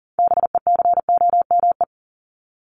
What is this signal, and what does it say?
Morse code, heard as a keyed tone, about 39 wpm, 700 Hz. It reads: BECOME
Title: Top 501-750 Words 40wpm